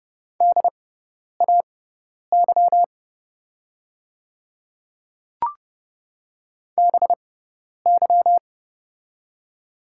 Morse code, heard as a keyed tone, about 30 wpm, spaced as for 10 wpm, 700 Hz. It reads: DAY BY
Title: 6x Character Spacing - Top 100 Words - 30wpm